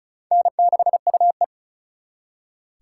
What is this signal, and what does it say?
Morse code, heard as a keyed tone, about 47 wpm, spaced as for 41 wpm, 700 Hz. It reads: N6UE
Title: Speed Racing - USA 1x2 Call Signs 35wpm